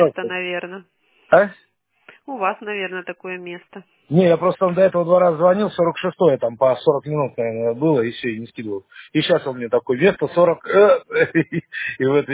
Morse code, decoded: none

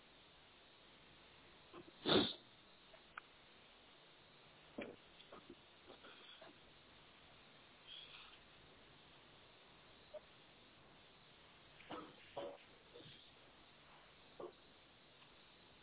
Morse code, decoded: none

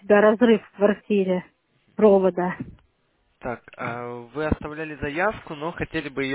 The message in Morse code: none

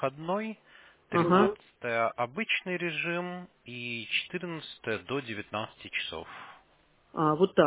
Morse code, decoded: none

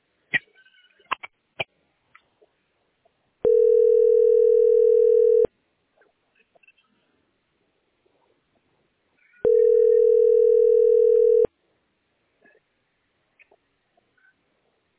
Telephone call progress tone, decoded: ringback tone

